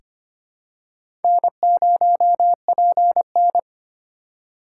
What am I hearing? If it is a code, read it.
Morse code, heard as a keyed tone, about 25 wpm, 700 Hz. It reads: N0PN
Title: USA Call Signs - 3x Character Spacing - 25wpm